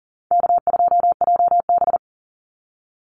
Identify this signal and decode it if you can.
Morse code, heard as a keyed tone, about 38 wpm, 700 Hz. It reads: K2JB
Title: NA SOTA Chaser Callsigns 40wpm